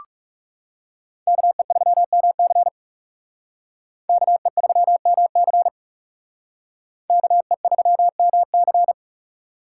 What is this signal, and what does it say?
Morse code, heard as a keyed tone, about 37 wpm, 700 Hz. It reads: KE3MC KE3MC KE3MC
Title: Speed Racing - USA 2x2 Call Signs 30wpm